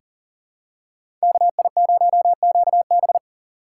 Morse code, heard as a keyed tone, about 40 wpm, 700 Hz. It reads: KI0QB
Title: USA Call Signs - 5 Characters - 4x Character Spacing - 40wpm